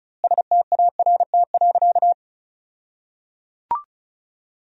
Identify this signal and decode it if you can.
Morse code, heard as a keyed tone, about 36 wpm, 700 Hz. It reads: START.
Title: 2x Word-Spacing - Sentences from Top 300 Words - 35wpm